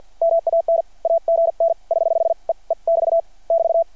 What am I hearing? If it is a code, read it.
Morse code, heard as a keyed tone, about 53 wpm, 670 Hz. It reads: MAN AGN <HH> E E <BT> <BT>
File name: archived / CRSnet